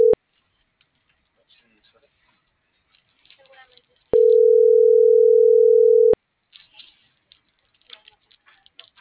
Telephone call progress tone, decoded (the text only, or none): ringback tone